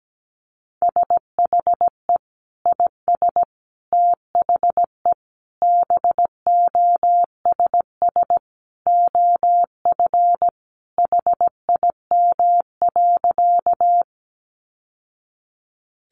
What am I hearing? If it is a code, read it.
Morse code, heard as a keyed tone, about 17 wpm, 700 Hz. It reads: SHE IS THE BOSS OF HIM.
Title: Sentences from Top 700 Words 17wpm